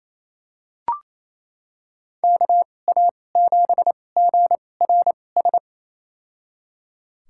Morse code, heard as a keyed tone, about 28 wpm, spaced as for 20 wpm, 700 Hz. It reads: KA7GRS
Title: USA Call Signs - 2x Character Spacing - 28wpm